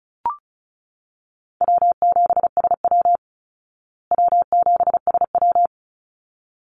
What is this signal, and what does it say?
Morse code, heard as a keyed tone, about 34 wpm, 700 Hz. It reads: W7SW W7SW
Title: Easy Call Signs Repeated 2x - 35wpm